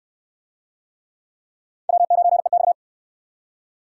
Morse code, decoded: W9EX

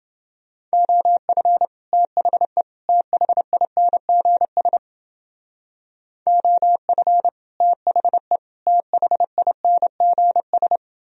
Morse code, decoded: OF THE THINGS OF THE THINGS

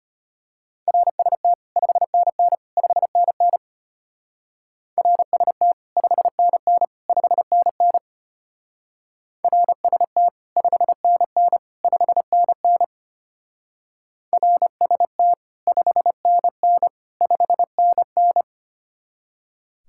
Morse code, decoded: RST 5NN 5NN RST 5NN 5NN RST 5NN 5NN RST 5NN 5NN

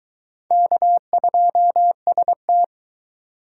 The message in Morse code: K2ST

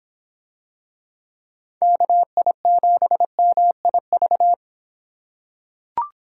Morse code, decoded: KI7MIV

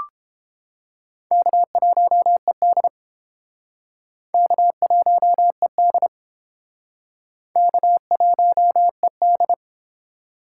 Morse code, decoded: K1ED K1ED K1ED

